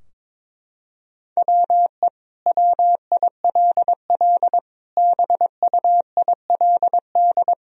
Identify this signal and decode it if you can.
Morse code, heard as a keyed tone, about 22 wpm, 700 Hz. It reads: WE WILL BUILD
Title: Sentences from Top 1000 Words - Review Part 4 - 22wpm